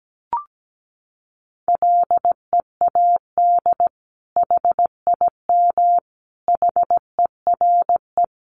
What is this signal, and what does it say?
Morse code, heard as a keyed tone, about 17 wpm, 700 Hz. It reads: LEAD HIM HERE
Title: Sentences from Top 1200 Words 17wpm